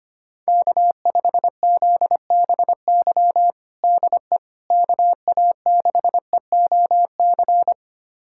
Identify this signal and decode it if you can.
Morse code, heard as a keyed tone, about 25 wpm, 700 Hz. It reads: K5ZBY DE KA6EOC
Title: USA Callsign DE Callsign 25wpm